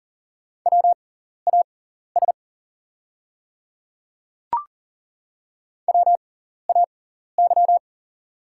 Morse code, heard as a keyed tone, about 40 wpm, spaced as for 14 wpm, 690 Hz. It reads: WAS WAY
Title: 6x Character Spacing - Top 300 Words - 40wpm